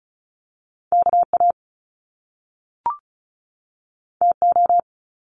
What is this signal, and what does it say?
Morse code, heard as a keyed tone, about 36 wpm, 700 Hz. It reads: KA TO
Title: Top Two Letter Combinations in Words 35wpm